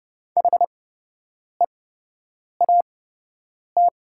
Morse code, heard as a keyed tone, about 30 wpm, spaced as for 8 wpm, 700 Hz. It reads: HEAT
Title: 8x Character Spacing - Top 500 Words - 30wpm